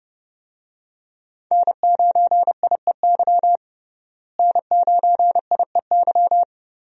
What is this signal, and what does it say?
Morse code, heard as a keyed tone, about 32 wpm, 700 Hz. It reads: N9IEY N9IEY
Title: USA Call Signs - 5 Characters - Repeated 2x - 30wpm